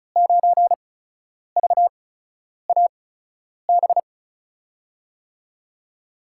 Morse code, heard as a keyed tone, about 36 wpm, spaced as for 10 wpm, 700 Hz. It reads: 9VAB
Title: USA Call Signs - 5 Characters - 8x Character Spacing - 35wpm